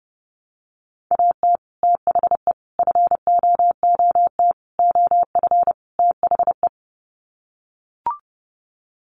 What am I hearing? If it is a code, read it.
Morse code, heard as a keyed tone, about 30 wpm, 700 Hz. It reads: AT THE FOOT OF THE
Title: Sets of 5 Words 30wpm